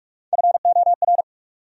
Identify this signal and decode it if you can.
Morse code, heard as a keyed tone, about 60 wpm, spaced as for 52 wpm, 700 Hz. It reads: FOR
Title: Speed-Racing Top 300 Words 45wpm